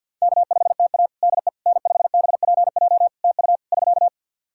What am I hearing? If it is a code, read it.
Morse code, heard as a keyed tone, about 48 wpm, 700 Hz. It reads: K5TA DE N5BPJ TU <SK>
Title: K1USN SST QSO Practice 50wpm